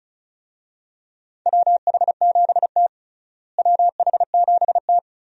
Morse code, heard as a keyed tone, about 36 wpm, 700 Hz. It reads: WH7T WH7T
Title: USA Call Signs - 4 Characters - Repeated 2x - 35wpm